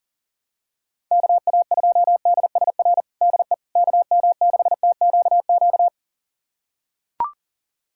Morse code, heard as a keyed tone, about 40 wpm, 700 Hz. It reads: KA2DSR DE KM6TQQ
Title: USA Callsign DE Callsign 40wpm